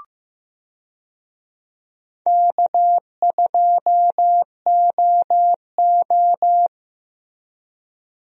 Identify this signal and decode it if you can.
Morse code, heard as a keyed tone, about 15 wpm, 700 Hz. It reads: K2OO
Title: USA Call Signs - 4 Characters - Repeated 2x - 15wpm